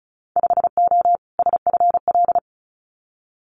Morse code, heard as a keyed tone, about 34 wpm, 700 Hz. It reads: 5O SFL
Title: Winter Field Day Exchanges 35wpm